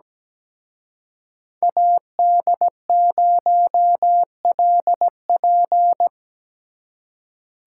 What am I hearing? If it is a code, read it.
Morse code, heard as a keyed tone, about 20 wpm, 700 Hz. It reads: AD0LP
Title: Speed Racing - USA 2x2 Call Signs 17wpm